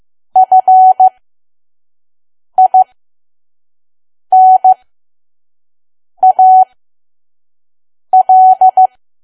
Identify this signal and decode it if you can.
Morse code, heard as a keyed tone, about 15 wpm, spaced as for 5 wpm, 750 Hz. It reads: FINAL
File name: Archive / 5 WPM files / PM